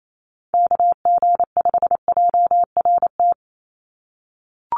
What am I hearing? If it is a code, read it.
Morse code, heard as a keyed tone, about 28 wpm, 700 Hz. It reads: KG5JRT
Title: USA Call Signs 28wpm